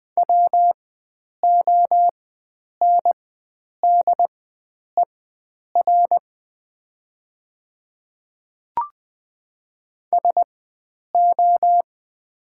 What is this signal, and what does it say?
Morse code, heard as a keyed tone, about 20 wpm, spaced as for 9 wpm, 700 Hz. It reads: WONDER SO
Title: 4x Character Spacing - Top 500 Words - 20wpm